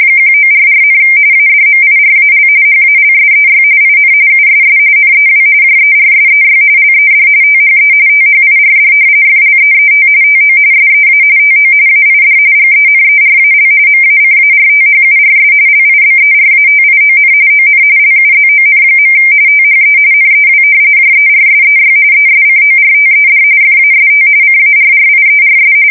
RTTY, about 45.45 baud. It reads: 1944, ADMIRAL HALSEY IN USS NEW JERSEY LED THE THIRD FLEET TO SEA TO AID GENERAL MACARTHURS LIBERATION ON LUZON. THE NEW JERSEY ESCORTED FAST CARRIERS
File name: fsk170 / battleshipnj